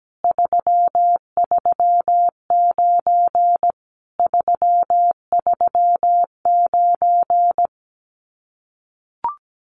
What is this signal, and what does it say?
Morse code, heard as a keyed tone, about 17 wpm, 700 Hz. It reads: 339 339
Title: Common QSO Phrases Encore 17wpm